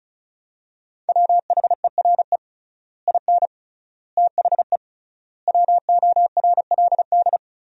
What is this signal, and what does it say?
Morse code, heard as a keyed tone, about 34 wpm, 700 Hz. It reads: WHERE IN THE WORLD
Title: 3x Word-Spacing - Sentences from Top 200 Words - 35wpm